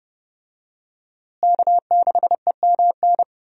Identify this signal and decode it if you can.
Morse code, heard as a keyed tone, about 30 wpm, 700 Hz. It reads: K6EMN
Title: USA Call Signs - 5 Characters - Repeated 2x - 30wpm